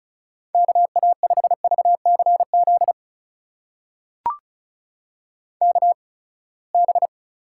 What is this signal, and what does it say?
Morse code, heard as a keyed tone, about 36 wpm, spaced as for 24 wpm, 700 Hz. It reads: KA5VCZ KB
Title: USA Call Signs - 8x Character Spacing - 35wpm